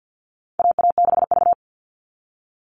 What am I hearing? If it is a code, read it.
Morse code, heard as a keyed tone, about 50 wpm, 700 Hz. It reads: AA6V
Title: All Cut Numbers - 4-Digit 50wpm